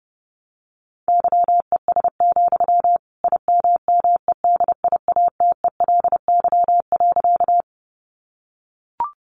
Morse code, heard as a keyed tone, about 28 wpm, 700 Hz. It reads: YES, IMMEDIATELY.
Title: Sentences from Top 1100 Words 30wpm